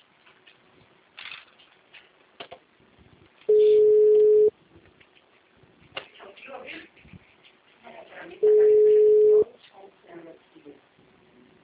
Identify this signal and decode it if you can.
Ringback tone (Europe, 425 Hz), with speech.
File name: out- -2002- - - .67636